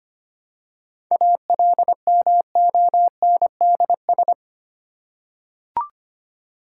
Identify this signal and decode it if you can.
Morse code, heard as a keyed tone, about 25 wpm, 700 Hz. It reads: ALMONDS
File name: practice / 25wpm / Difficult Words with Hints - 25wpm